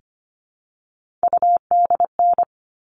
Morse code, heard as a keyed tone, about 25 wpm, 700 Hz. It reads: UDN
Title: All Cut Numbers - 3-Digit 25wpm